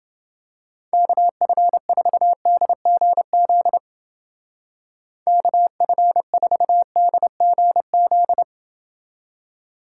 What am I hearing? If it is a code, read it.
Morse code, heard as a keyed tone, about 29 wpm, 700 Hz. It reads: KF4DGZ KF4DGZ